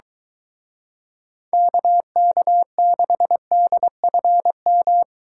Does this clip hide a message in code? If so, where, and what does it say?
Yes: Morse code from 1.5 to 5.0 s, heard as a keyed tone, about 23 wpm, 700 Hz. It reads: KK6DFM